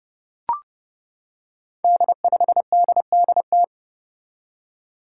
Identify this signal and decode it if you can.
Morse code, heard as a keyed tone, about 32 wpm, 700 Hz. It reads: D5DDT
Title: Five Challenging Letter-Number Combinations 30wpm